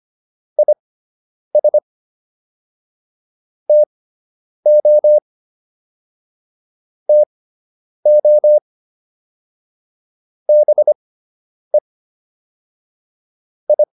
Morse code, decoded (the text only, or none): IS TO TO BE I